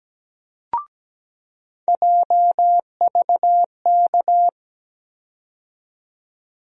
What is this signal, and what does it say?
Morse code, heard as a keyed tone, about 17 wpm, 700 Hz. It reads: JVK
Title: USA Call Sign Postfixes - 3 Letters 17wpm